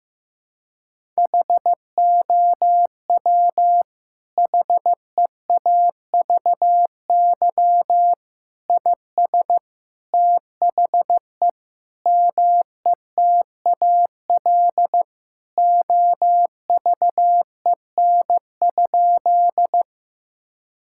Morse code, heard as a keyed tone, about 15 wpm, 700 Hz. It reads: HOW HEAVY IS THE METAL OVEN?
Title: Sentences from Top 1000 Words - Review Part 6 - 15wpm